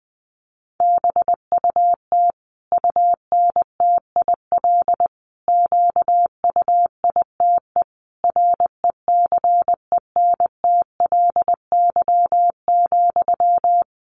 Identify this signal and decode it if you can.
Morse code, heard as a keyed tone, about 19 wpm, 700 Hz. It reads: BUT UNTIL QUITE RECENTLY,
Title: Sentences from Top 1200 Words 20wpm